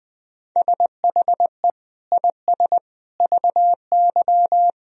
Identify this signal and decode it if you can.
Morse code, heard as a keyed tone, about 20 wpm, 700 Hz. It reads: SHE IS VY
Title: Sentences from Top 400 Words and Abbreviations 20wpm